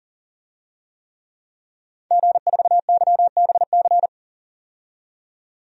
Morse code, heard as a keyed tone, about 40 wpm, 700 Hz. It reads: G4YBC
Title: DXCC-100 Call Signs - 5 Characters - 40wpm